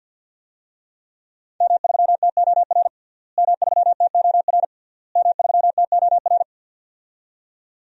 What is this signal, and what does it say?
Morse code, heard as a keyed tone, about 50 wpm, 700 Hz. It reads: M3TOR M3TOR M3TOR
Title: DXCC-100 Call Signs - 5 Characters - Repeated 3x - 50wpm